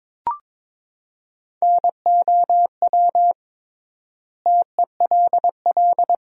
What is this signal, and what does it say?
Morse code, heard as a keyed tone, about 22 wpm, 700 Hz. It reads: NOW TELL
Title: 3x Word-Spacing - Sentences from Top 200 Words - 22wpm